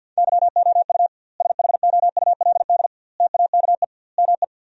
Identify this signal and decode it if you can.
Morse code, heard as a keyed tone, about 50 wpm, 700 Hz. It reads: YOU SHOULD TAKE ME
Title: 2x Word-Spacing - Sentences from Top 300 Words - 50wpm